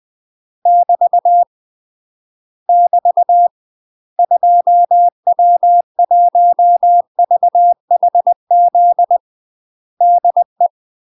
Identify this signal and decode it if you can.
Morse code, heard as a keyed tone, about 20 wpm, 700 Hz. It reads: <BT> <BT> 2W1VHZ DE
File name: UK MASTER QSOs TRAINING / 20WPM / UK MASTER QSOs 0003